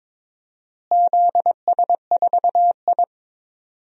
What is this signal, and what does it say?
Morse code, heard as a keyed tone, about 22 wpm, 700 Hz. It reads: ZS4I
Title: DXCC-100 Call Signs - 4 Characters - Repeated 2x - 22wpm